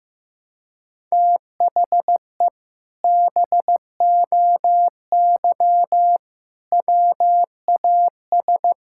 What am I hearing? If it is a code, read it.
Morse code, heard as a keyed tone, about 16 wpm, 700 Hz. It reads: THE BOY WAS
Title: Sentences from Top 1000 Words - Review Part 6 - 15wpm